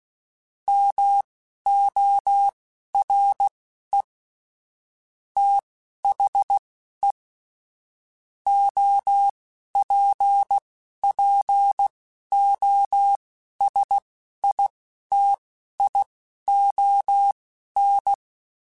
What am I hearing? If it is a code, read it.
Morse code, heard as a keyed tone, about 16 wpm, spaced as for 11 wpm, 800 Hz. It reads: MORE THE OPPOSITION
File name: Technical Subjects / Technical Subjects-mp3s / XC @10